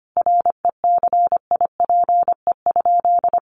Morse code, heard as a keyed tone, about 25 wpm, 700 Hz. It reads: RECIPE?
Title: Sentences from Top 1200 Words 25wpm